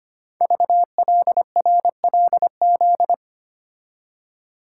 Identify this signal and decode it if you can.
Morse code, heard as a keyed tone, about 25 wpm, 700 Hz. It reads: VLRLZ
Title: Code Groups - 5 Letters - 25wpm